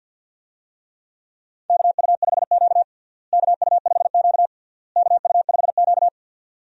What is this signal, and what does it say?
Morse code, heard as a keyed tone, about 51 wpm, 700 Hz. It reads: KU5Q KU5Q KU5Q
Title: USA Call Signs - 4 Characters - Repeated 3x - 50wpm